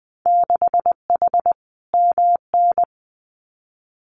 Morse code, heard as a keyed tone, about 20 wpm, 700 Hz. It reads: 6H MN
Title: Winter Field Day Exchanges Encore 20wpm